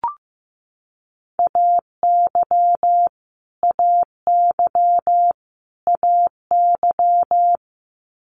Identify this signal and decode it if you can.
Morse code, heard as a keyed tone, about 15 wpm, 700 Hz. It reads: AY AY AY
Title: Multi-letter Phonemes Repeated 3x 15wpm